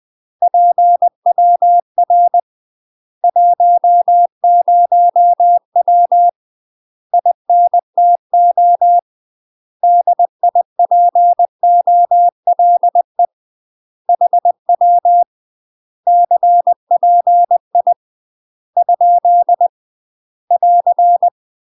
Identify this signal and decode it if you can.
Morse code, heard as a keyed tone, about 20 wpm, 700 Hz. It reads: PWR 10W INTO DIPOLE HW CPI ? <AR>
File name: UK MASTER QSOs TRAINING / 20WPM / UK MASTER QSOs 0022